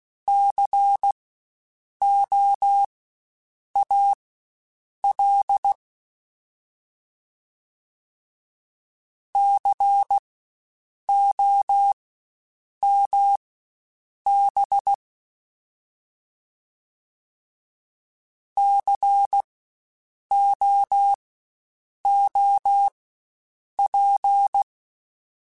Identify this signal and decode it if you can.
Morse code, heard as a keyed tone, about 16 wpm, spaced as for 7 wpm, 800 Hz. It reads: COAL COMB COOP
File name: Spaced Alphas / Spaced Alphas mp3s / Cs sp @15